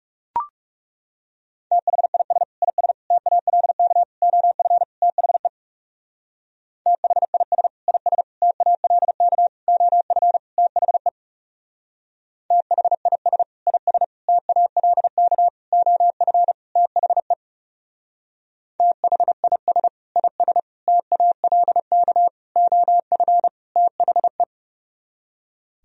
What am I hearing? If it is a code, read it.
Morse code, heard as a keyed tone, about 37 wpm, 700 Hz. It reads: THIS IS TALK OF THE THIS IS TALK OF THE THIS IS TALK OF THE THIS IS TALK OF THE
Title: Speed-Racing - Sets of 5 Words 30wpm